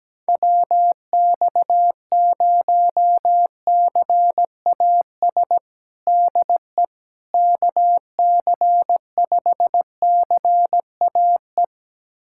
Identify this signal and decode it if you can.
Morse code, heard as a keyed tone, about 17 wpm, 700 Hz. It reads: WX0CAS DE KC5CAE
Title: USA Callsign DE Callsign - 6 Characters - 17wpm